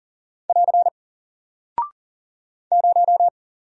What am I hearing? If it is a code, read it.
Morse code, heard as a keyed tone, about 40 wpm, 700 Hz. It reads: <AR> 0
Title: Single Character (Letter:Number:Punctuation:Prosign) Alt 40wpm